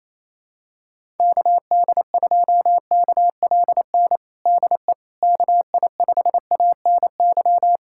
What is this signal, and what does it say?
Morse code, heard as a keyed tone, about 28 wpm, 700 Hz. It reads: KD2KLN DE KI5ANY
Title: USA Callsign DE Callsign - 6 Characters - 28wpm